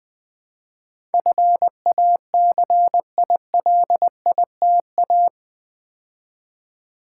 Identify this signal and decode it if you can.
Morse code, heard as a keyed tone, about 20 wpm, 700 Hz. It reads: FACILITA
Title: Build A Word - 4 Letters At A Time - 10-Letter Words - 20wpm